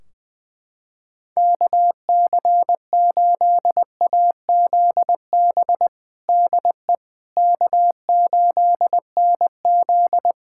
Morse code, heard as a keyed tone, about 20 wpm, 700 Hz. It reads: KC8AZB DE K8NZ
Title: Practice QSO - Exchange 4 - Ending - 20wpm